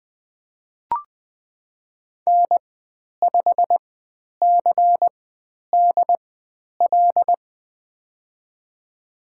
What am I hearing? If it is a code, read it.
Morse code, heard as a keyed tone, about 20 wpm, spaced as for 10 wpm, 700 Hz. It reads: N5CDL
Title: 102 Lesson 17g - Letter u - Call Signs 20wpm